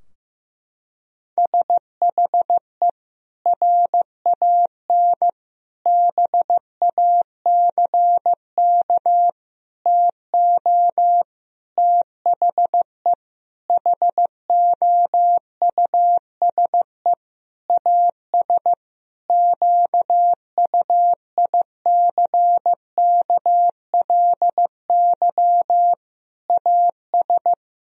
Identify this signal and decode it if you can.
Morse code, heard as a keyed tone, about 16 wpm, 700 Hz. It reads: SHE RAN BACK TO THE HOUSE AS QUICKLY AS
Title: Sentences from Top 1000 Words - Review Part 3 - 15wpm